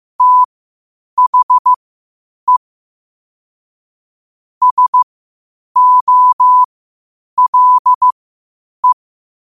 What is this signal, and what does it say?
Morse code, heard as a keyed tone, about 15 wpm, spaced as for 8 wpm, 1000 Hz. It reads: THE SOLE
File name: qotdfiles / 8wpm